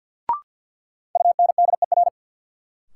Morse code, heard as a keyed tone, about 53 wpm, 700 Hz. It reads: UNDER
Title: Sets of 2 Words - New Word Drill - 50wpm